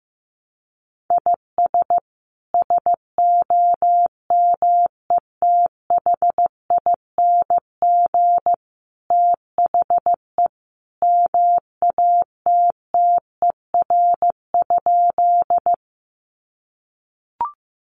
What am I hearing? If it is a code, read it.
Morse code, heard as a keyed tone, about 15 wpm, 700 Hz. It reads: IS SOMETHING THE MATTER?
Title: Sentences from Top 700 Words 15wpm